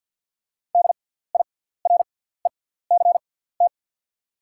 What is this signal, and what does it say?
Morse code, heard as a keyed tone, about 50 wpm, spaced as for 17 wpm, 690 Hz. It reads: DIRECT
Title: 6x Character Spacing - Top 500 Words - 50wpm